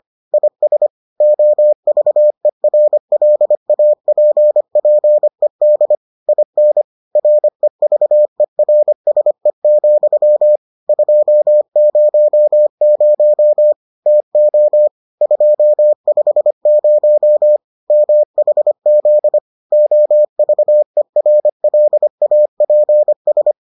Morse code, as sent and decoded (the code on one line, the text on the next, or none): .. ... / --- ...- . .-. .-.. .- .--. .--. -.. / .. -. / .-. ...- . .-. ... --..-- / ..--- ----- ----- / - --- / ..--- ..... ----- / -- .... --.. / --- ...- . .-. .-.. .- .--. ...
IS OVERLAPPED IN REVERSE, 200 TO 250 MHZ OVERLAPS